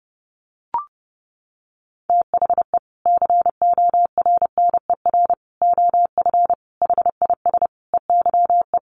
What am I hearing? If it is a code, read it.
Morse code, heard as a keyed tone, about 30 wpm, 700 Hz. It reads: THE CORNER OF HIS EYE
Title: Sets of 5 Words Encore2 30wpm